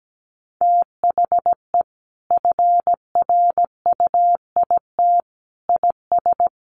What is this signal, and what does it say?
Morse code, heard as a keyed tone, about 17 wpm, 700 Hz. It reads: THE FRUIT IS
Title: Sentences from Top 1300 Words 17wpm